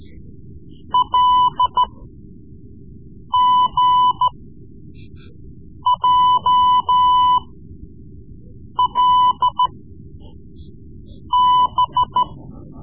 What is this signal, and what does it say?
Morse code, heard as a keyed tone, about 13 wpm, spaced as for 5 wpm, 1000 Hz. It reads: LGJLB
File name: Morse Lessons / Morse Lessons / Tape-3B